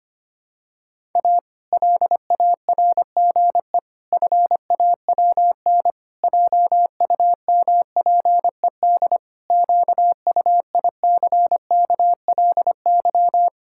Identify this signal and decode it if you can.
Morse code, heard as a keyed tone, about 25 wpm, 700 Hz. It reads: A LARGE FAWN JUMPED QUICKLY
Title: Pangram Challenge v2 - 25wpm